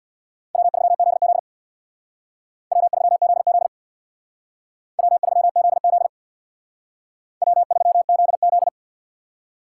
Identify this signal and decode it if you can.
Morse code, heard as a keyed tone, about 60 wpm, 700 Hz. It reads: W3ZZ W3ZZ W3ZZ W3ZZ
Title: Speed Racing - USA 1x2 Call Signs 50wpm